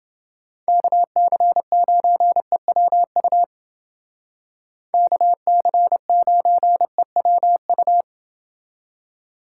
Morse code, heard as a keyed tone, about 29 wpm, 700 Hz. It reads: KC9EWU KC9EWU